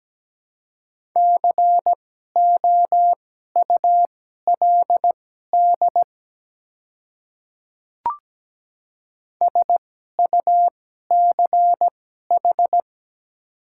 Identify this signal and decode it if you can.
Morse code, heard as a keyed tone, about 17 wpm, spaced as for 12 wpm, 700 Hz. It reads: COULD SUCH
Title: 2x Character Spacing - Top 200 Words - 17wpm